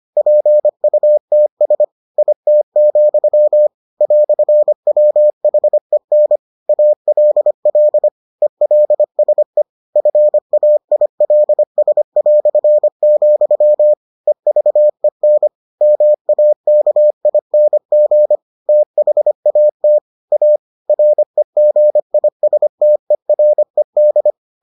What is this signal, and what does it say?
Morse code, heard as a keyed tone, about 25 wpm, 600 Hz. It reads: PUTS IT, "WHEN ALL ELSE FAILS", EVEN MAKING THAT A REGISTERED